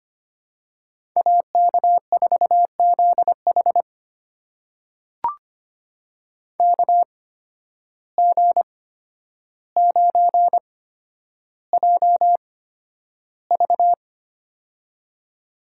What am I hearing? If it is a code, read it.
Morse code, heard as a keyed tone, about 25 wpm, spaced as for 11 wpm, 700 Hz. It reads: AK4ZH KG9JV
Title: USA Call Signs - 5 Characters - 8x Character Spacing - 25wpm